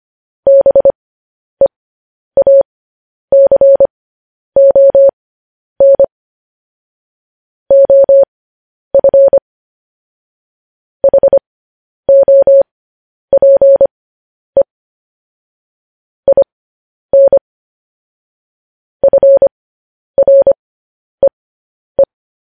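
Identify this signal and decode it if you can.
Morse code, heard as a keyed tone, about 25 wpm, spaced as for 10 wpm, 550 Hz. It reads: BEACON OF HOPE IN FREE
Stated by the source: Morse code news headlines at 10.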